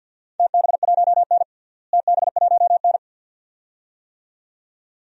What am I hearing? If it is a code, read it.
Morse code, heard as a keyed tone, about 51 wpm, 700 Hz. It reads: TB1N TB1N
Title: DXCC-100 Call Signs - 4 Characters - Repeated 2x - 50wpm